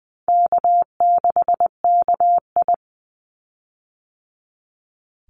Morse code, heard as a keyed tone, about 20 wpm, 700 Hz. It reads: K6KI
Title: Call Signs - 20wpm